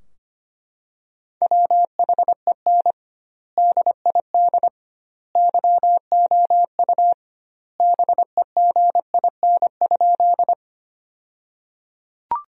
Morse code, heard as a keyed tone, about 25 wpm, 700 Hz. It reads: WHEN DID YOU BEGIN?
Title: 2x Word-Spacing - Sentences from Top 300 Words - 25wpm